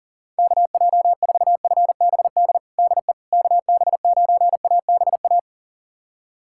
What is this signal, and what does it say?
Morse code, heard as a keyed tone, about 37 wpm, 700 Hz. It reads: KJ4FBD DE KB9ABA
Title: USA Callsign DE Callsign - 6 Characters - 40wpm